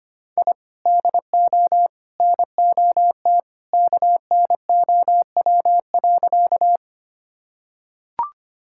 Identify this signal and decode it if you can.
Morse code, heard as a keyed tone, about 25 wpm, 700 Hz. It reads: I DO NOT KNOW.
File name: practice / 25wpm / Sentences with 3-Word Phrases - 25wpm